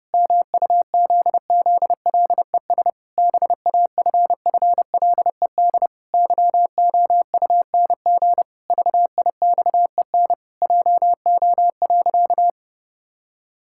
Morse code, heard as a keyed tone, about 30 wpm, 700 Hz. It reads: MUZZLES BAFFLED YOUNG VIXEN JO.
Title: Pangram Challenge v2 - 30wpm